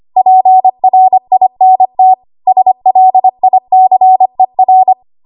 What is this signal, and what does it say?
Morse code, heard as a keyed tone, about 25 wpm, 750 Hz. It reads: PRINT SLICER
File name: Archive / 25 WPM / 25WPM